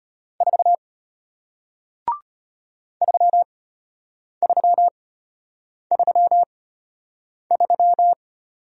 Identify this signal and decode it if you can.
Morse code, heard as a keyed tone, about 34 wpm, spaced as for 8 wpm, 700 Hz. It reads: 4 3333